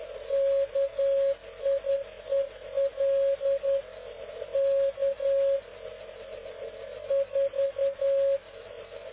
Morse code, heard as a keyed tone, about 10 wpm, 550 Hz. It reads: KIEL K 4
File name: cw / ubacw dk0wcy slow